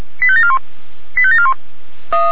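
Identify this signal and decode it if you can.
Five-tone selective call (ZVEI), heard as quick runs of five tones, tones of about 70 ms. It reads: 85741 85741
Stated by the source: ZVEI five-tone call 85722 received by SDR, channel 464.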